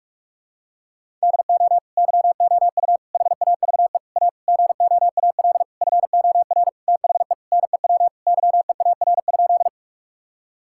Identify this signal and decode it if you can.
Morse code, heard as a keyed tone, about 48 wpm, 700 Hz. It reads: DO YOU HAVE A GOAL FOR THE NEW YEAR?
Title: Sentences from Top 1000 Words - Review Part 5 - 45wpm